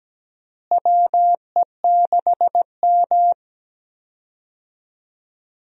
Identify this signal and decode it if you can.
Morse code, heard as a keyed tone, about 17 wpm, 700 Hz. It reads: WE6M
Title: USA Call Signs - 4 Characters - Repeated 3x - 17wpm